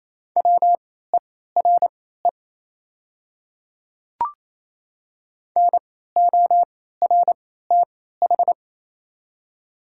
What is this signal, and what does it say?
Morse code, heard as a keyed tone, about 28 wpm, spaced as for 16 wpm, 690 Hz. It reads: WERE NORTH
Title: 3x Character Spacing - Top 300 Words - 28wpm